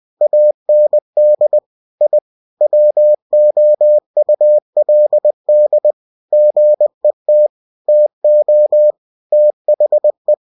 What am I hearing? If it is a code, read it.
Morse code, heard as a keyed tone, about 20 wpm, 600 Hz. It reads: AND I WOULD GET TO THE
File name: cw-books / winnie-the-pooh / winnie-20wpm-page-0036